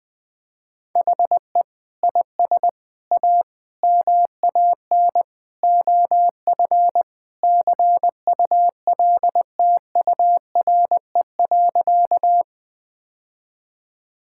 Morse code, heard as a keyed tone, about 21 wpm, 700 Hz. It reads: HE IS A MAN OF CULTURE.
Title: Sentences from Top 1000 Words - Review Part 6 - 20wpm